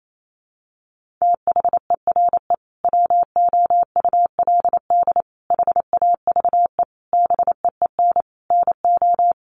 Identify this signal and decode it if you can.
Morse code, heard as a keyed tone, about 28 wpm, 700 Hz. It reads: THERE WOULD HAVE BEEN NO
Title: Sentences from Top 1400 Words 28wpm